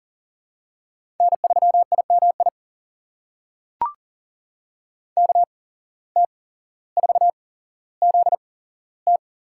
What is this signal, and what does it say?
Morse code, heard as a keyed tone, about 41 wpm, spaced as for 17 wpm, 700 Hz. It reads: N3IMI KT4ZT
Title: USA Call Signs - 5 Characters - 8x Character Spacing - 40wpm